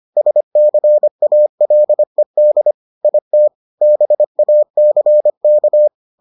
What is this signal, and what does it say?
Morse code, heard as a keyed tone, about 25 wpm, 600 Hz. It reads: SCALED IT BACK